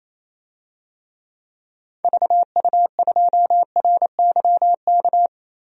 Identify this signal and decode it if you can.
Morse code, heard as a keyed tone, about 28 wpm, 700 Hz. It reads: VU2RYK
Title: DXCC-100 Call Signs - 6 Characters - Repeated 3x - 28wpm